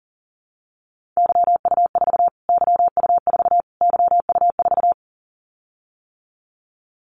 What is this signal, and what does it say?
Morse code, heard as a keyed tone, about 40 wpm, 700 Hz. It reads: YU4 YU4 YU4
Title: Callsign Prefixes for Top 25 DXCC - Repeated 3x - 40wpm